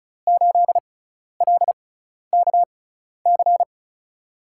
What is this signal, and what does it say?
Morse code, heard as a keyed tone, about 36 wpm, spaced as for 12 wpm, 700 Hz. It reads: 8LKC
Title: USA Call Signs - 6 Characters - 6x Character Spacing - 35wpm